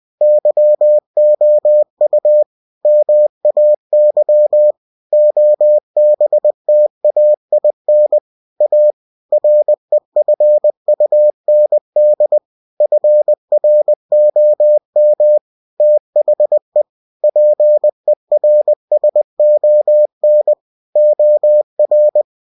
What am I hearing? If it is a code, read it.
Morse code, heard as a keyed tone, about 20 wpm, 600 Hz. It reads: YOU MAY OBTAIN A REFUND FROM THE PERSON OR
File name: cw-books / winnie-the-pooh / winnie-20wpm-page-0113